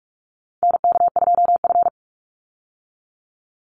Morse code, heard as a keyed tone, about 44 wpm, 700 Hz. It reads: NK2F
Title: Call Signs Repeated 3x - 45wpm